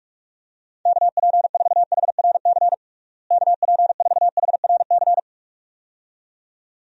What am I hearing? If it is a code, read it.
Morse code, heard as a keyed tone, about 41 wpm, 700 Hz. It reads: KP4HRC KP4HRC
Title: USA Call Signs - 6 Characters - Repeated 2x - 45wpm